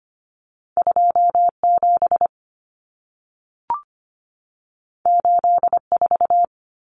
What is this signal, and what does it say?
Morse code, heard as a keyed tone, about 25 wpm, 700 Hz. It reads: 27 84
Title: Numbers - 2-Digit 25wpm